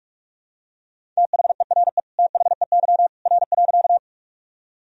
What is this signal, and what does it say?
Morse code, heard as a keyed tone, about 41 wpm, 700 Hz. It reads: THERE THEY R.
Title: Sentences from Top 100 Words and Abbreviations 45wpm